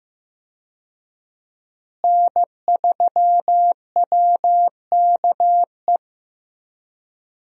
Morse code, heard as a keyed tone, about 15 wpm, 700 Hz. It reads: N3WKE
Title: USA Call Signs - 5 Characters - Repeated 2x - 15wpm